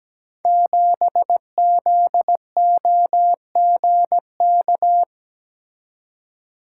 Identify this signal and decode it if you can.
Morse code, heard as a keyed tone, about 17 wpm, 700 Hz. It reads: 7ZOGK
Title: Code Groups - 5 Letter:Numbers - 17wpm